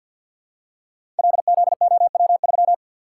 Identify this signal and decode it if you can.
Morse code, heard as a keyed tone, about 51 wpm, 700 Hz. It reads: LZOW3
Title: Code Groups - 5 Letter:Numbers - 50wpm